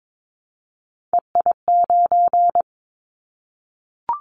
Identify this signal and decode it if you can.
Morse code, heard as a keyed tone, about 22 wpm, 700 Hz. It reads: EI9
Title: Callsign Prefixes for Top 50 DXCC - 22wpm